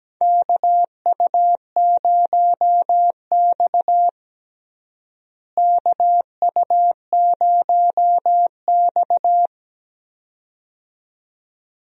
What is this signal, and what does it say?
Morse code, heard as a keyed tone, about 17 wpm, 700 Hz. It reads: KU0X KU0X
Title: USA Call Signs - 4 Characters - Repeated 2x - 17wpm